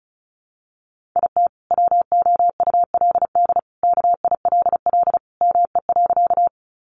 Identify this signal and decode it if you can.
Morse code, heard as a keyed tone, about 34 wpm, 700 Hz. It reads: IT WOULD KILL ME.
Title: Sentences from Top 1400 Words 35wpm